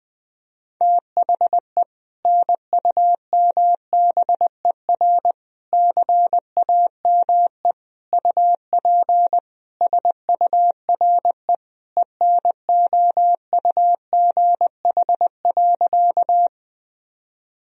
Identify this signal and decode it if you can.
Morse code, heard as a keyed tone, about 21 wpm, 700 Hz. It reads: THE NUMBER CAME UP SURE ENOUGH.
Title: Sentences from Top 1000 Words - Review Part 1 - 20wpm